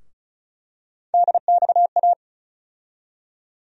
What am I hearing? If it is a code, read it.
Morse code, heard as a keyed tone, about 36 wpm, 700 Hz. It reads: DXA
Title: USA Call Sign Postfixes - 3 Letters 35wpm